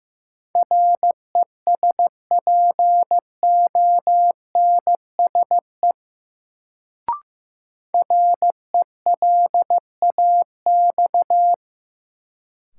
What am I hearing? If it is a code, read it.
Morse code, heard as a keyed tone, about 15 wpm, 700 Hz. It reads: RESPONSE RELAX
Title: Sentences from Top 1400 Words - New Word Drill - 15wpm